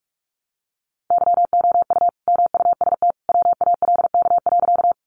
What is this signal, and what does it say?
Morse code, heard as a keyed tone, about 44 wpm, 700 Hz. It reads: YOU MUST WALK.
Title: Sentences from Top 300 Words 45wpm